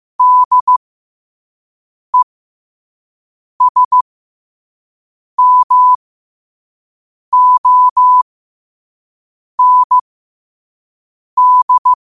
Morse code, heard as a keyed tone, about 15 wpm, spaced as for 5 wpm, 1000 Hz. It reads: DESMOND
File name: qotdfiles / 5wpm